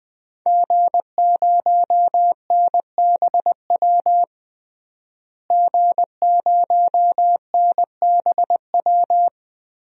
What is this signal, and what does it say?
Morse code, heard as a keyed tone, about 21 wpm, 700 Hz. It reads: G0NBW G0NBW
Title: DXCC-100 Call Signs - 5 Characters - Repeated 3x - 20wpm